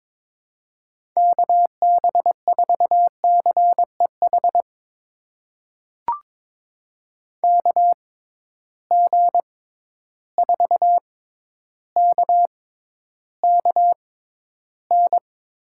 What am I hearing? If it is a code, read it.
Morse code, heard as a keyed tone, about 22 wpm, spaced as for 11 wpm, 700 Hz. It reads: KB4CEH KG4KKN
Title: USA Call Signs - 6 Characters - 6x Character Spacing - 22wpm